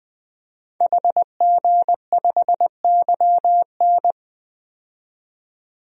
Morse code, heard as a keyed tone, about 20 wpm, 700 Hz. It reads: HG5YN
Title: DXCC-100 Call Signs - 5 Characters - 20wpm